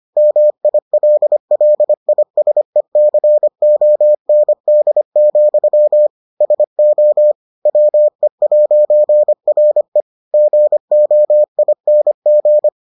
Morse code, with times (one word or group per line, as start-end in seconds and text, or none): 0.2-6.1 MILLISECOND,
6.4-7.3 SO
7.7-10.0 WE'RE
10.3-12.7 GOING